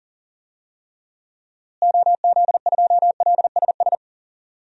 Morse code, heard as a keyed tone, about 40 wpm, 700 Hz. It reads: OZ2LSS
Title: DXCC-100 Call Signs - 6 Characters - Repeated 3x - 40wpm